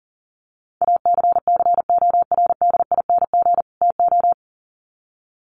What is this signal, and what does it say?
Morse code, heard as a keyed tone, about 40 wpm, 700 Hz. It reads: ACCORDING TO
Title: Sets of 2 Words 40wpm